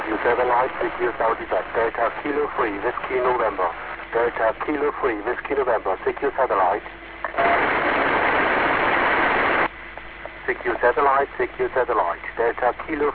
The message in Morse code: none